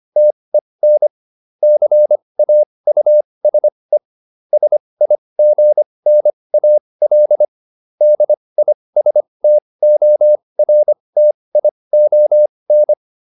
Morse code, heard as a keyed tone, about 25 wpm, spaced as for 20 wpm, 600 Hz. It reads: TEN CAUSE SIGNAL DISTORTION